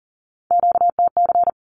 Morse code, heard as a keyed tone, about 40 wpm, 700 Hz. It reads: QTC
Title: Q-Codes 40wpm